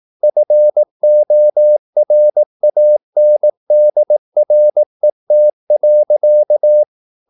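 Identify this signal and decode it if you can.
Morse code, heard as a keyed tone, about 18 wpm, 600 Hz. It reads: FORANDRET.
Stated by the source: Slow (QRS) Morse practice at 18 wpm, CW med Gnister.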